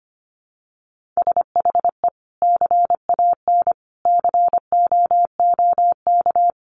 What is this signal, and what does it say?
Morse code, heard as a keyed tone, about 25 wpm, 700 Hz. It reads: SHE CAN COOK
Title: Sentences from Top 700 Words 25wpm